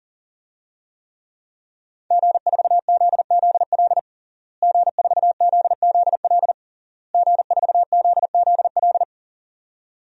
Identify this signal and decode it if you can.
Morse code, heard as a keyed tone, about 37 wpm, 700 Hz. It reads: G4ZZL G4ZZL G4ZZL
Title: DXCC-100 Call Signs - 5 Characters - Repeated 3x - 40wpm